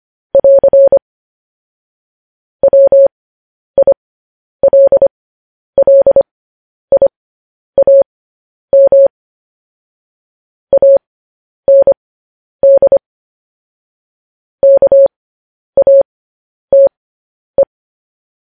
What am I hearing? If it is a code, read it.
Morse code, heard as a keyed tone, about 25 wpm, spaced as for 10 wpm, 550 Hz. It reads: <AR> WILLIAM AND KATE
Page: news headlines in morse code at 10 wpm